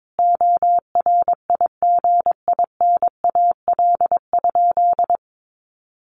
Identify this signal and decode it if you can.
Morse code, heard as a keyed tone, about 22 wpm, 700 Hz. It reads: ORIGINAL?
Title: Sentences from Top 900 Words 22wpm